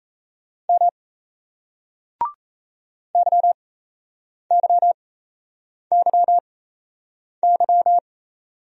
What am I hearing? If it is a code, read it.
Morse code, heard as a keyed tone, about 38 wpm, spaced as for 8 wpm, 700 Hz. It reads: M YYYY